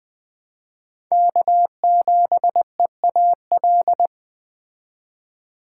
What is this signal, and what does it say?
Morse code, heard as a keyed tone, about 21 wpm, 700 Hz. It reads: K7EAL
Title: Speed Racing - USA 1x3 Call Signs 17wpm